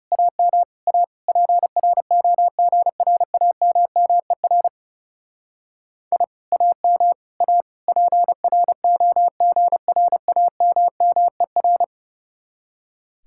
Morse code, heard as a keyed tone, about 36 wpm, 700 Hz. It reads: AM A PROGRAMMER I AM A PROGRAMMER
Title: Speed-Racing Common QSO Phrases - 30wpm